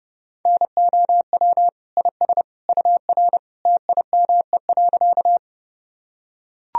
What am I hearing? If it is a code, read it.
Morse code, heard as a keyed tone, about 32 wpm, 700 Hz. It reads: NOW IS UR TIME.
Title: Sentences from Top 100 Words and Abbreviations 30wpm